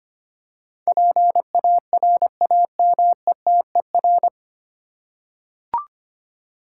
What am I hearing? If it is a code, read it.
Morse code, heard as a keyed tone, about 25 wpm, 700 Hz. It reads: PARAMETER
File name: practice / 25wpm / 9 Letter Words - First 3 Letters Given - 25wpm